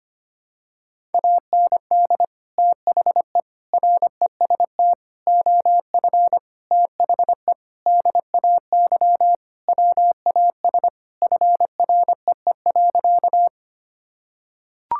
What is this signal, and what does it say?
Morse code, heard as a keyed tone, about 25 wpm, 700 Hz. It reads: AND THE REST OF THE DAY WAS FREE.
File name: practice / 25wpm / Sentences with 4-Word Phrases - 25wpm